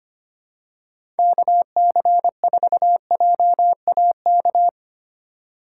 Morse code, heard as a keyed tone, about 25 wpm, 700 Hz. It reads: KC4JAK
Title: USA Call Signs - 6x Character Spacing - 25wpm